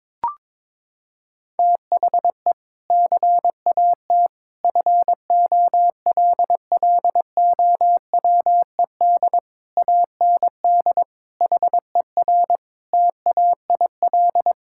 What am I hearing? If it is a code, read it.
Morse code, heard as a keyed tone, about 22 wpm, 700 Hz. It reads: THE CAT FOLLOWED AND HER TAIL
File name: practice / 22wpm / Sentences with 2-Word Phrases - 22wpm